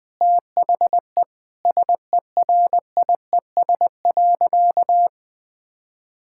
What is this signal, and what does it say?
Morse code, heard as a keyed tone, about 21 wpm, 700 Hz. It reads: THE SERIES.
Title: Sentences from Top 1000 Words - Review Part 5 - 20wpm